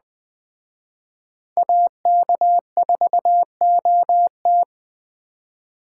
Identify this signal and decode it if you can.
Morse code, heard as a keyed tone, about 20 wpm, 700 Hz. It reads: AK4OT
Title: Speed Racing - USA 2x2 Call Signs 17wpm